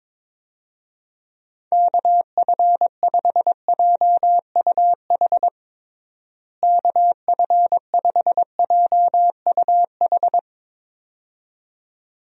Morse code, heard as a keyed tone, about 22 wpm, 700 Hz. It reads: KF5JUH KF5JUH
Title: USA Call Signs - 6 Characters - Repeated 2x - 22wpm